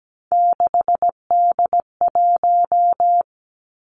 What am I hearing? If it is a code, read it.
Morse code, heard as a keyed tone, about 17 wpm, 700 Hz. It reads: 6D1
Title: Three Challenging Letter-Number Combinations 17wpm